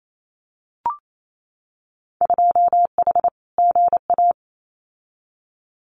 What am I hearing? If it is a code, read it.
Morse code, heard as a keyed tone, about 28 wpm, 700 Hz. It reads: 2H GA
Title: Winter Field Day Exchanges 28wpm